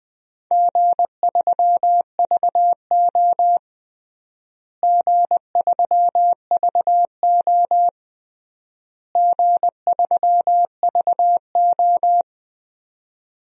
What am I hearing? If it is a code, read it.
Morse code, heard as a keyed tone, about 20 wpm, 700 Hz. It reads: G3VO G3VO G3VO
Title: DXCC-100 Call Signs - 4 Characters - Repeated 3x - 20wpm